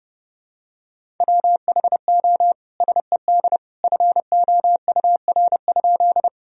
Morse code, heard as a keyed tone, about 32 wpm, 700 Hz. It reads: WHO SED FOUR?
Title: Sentences from Top 300 Words and Abbreviations 30wpm